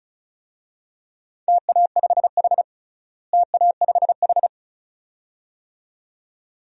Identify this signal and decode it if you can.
Morse code, heard as a keyed tone, about 36 wpm, 700 Hz. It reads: TA5H TA5H
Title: DXCC-100 Call Signs - 4 Characters - Repeated 2x - 35wpm